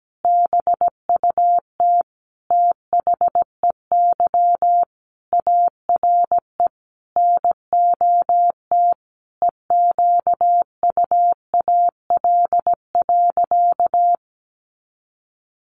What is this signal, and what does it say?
Morse code, heard as a keyed tone, about 17 wpm, 700 Hz. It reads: BUT THEY ARE NOT EQUAL.
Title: Sentences from Top 1400 Words 17wpm